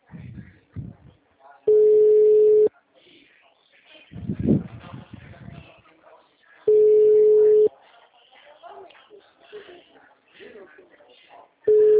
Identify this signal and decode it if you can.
Ringback tone (Europe, 425 Hz), with speech.
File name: out- -2008- - - .66864